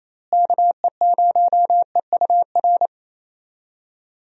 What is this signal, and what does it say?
Morse code, heard as a keyed tone, about 27 wpm, 700 Hz. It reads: KE0EUR
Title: USA Call Signs - 6 Characters - Repeated 2x - 28wpm